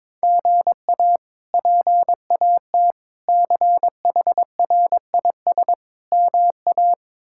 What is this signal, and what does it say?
Morse code, heard as a keyed tone, about 22 wpm, 700 Hz. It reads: GA PAT CHRIS MA
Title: K1USN SST QSO Practice - 22wpm